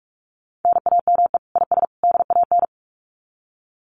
Morse code, heard as a keyed tone, about 44 wpm, 700 Hz. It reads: NAME IS DAN
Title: Common QSO Phrases 45wpm